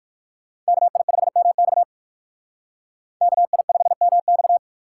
Morse code, heard as a keyed tone, about 54 wpm, 700 Hz. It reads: KI5MX KI5MX
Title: Speed Racing - USA 2x2 Call Signs 45wpm